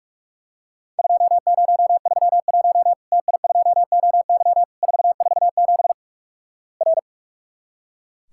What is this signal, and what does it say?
Morse code, heard as a keyed tone, about 44 wpm, 700 Hz. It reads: 2031 TI2OY 447 R
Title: Worked All Europe DX Contest - QTC Exchange - 45wpm